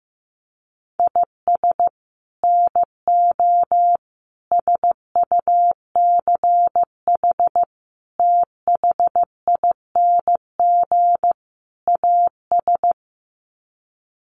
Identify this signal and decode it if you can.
Morse code, heard as a keyed tone, about 15 wpm, 700 Hz. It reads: IS NO SUCH THING AS
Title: Sets of 5 Words Encore 15wpm